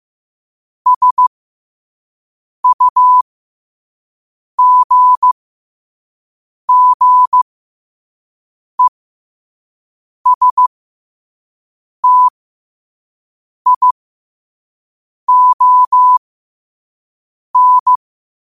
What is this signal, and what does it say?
Morse code, heard as a keyed tone, about 15 wpm, spaced as for 5 wpm, 1000 Hz. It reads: SUGGESTION
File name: qotdfiles / 5wpm